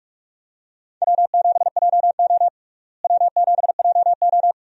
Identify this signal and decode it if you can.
Morse code, heard as a keyed tone, about 46 wpm, 700 Hz. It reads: W7JO W7JO
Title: USA Call Signs - 4 Characters - Repeated 2x - 45wpm